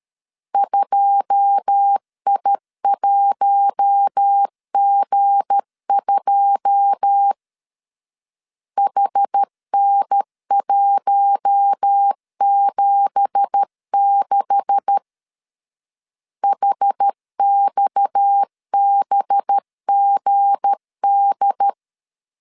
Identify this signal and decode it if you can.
Morse code, heard as a keyed tone, about 13 wpm, 800 Hz. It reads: 2I1G2 HN176 HXBGD
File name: CW / DET / 1 9 1500C S C 12wpm 3811